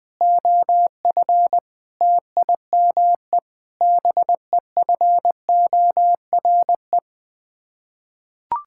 Morse code, heard as a keyed tone, about 20 wpm, 700 Hz. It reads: OF TIME BEFORE
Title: Sets of 8 Words - 20wpm